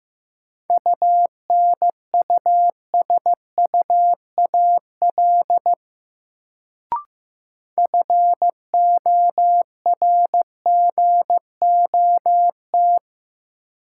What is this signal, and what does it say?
Morse code, heard as a keyed tone, about 15 wpm, 700 Hz. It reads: UNUSUAL FORGOT
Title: Sentences from Top 1400 Words - New Word Drill - 15wpm